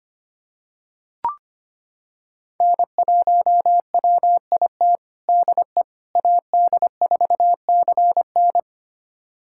Morse code, heard as a keyed tone, about 25 wpm, 700 Hz. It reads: N1WIT DE AD4CN
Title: USA Callsign DE Callsign - 5 Characters - 25wpm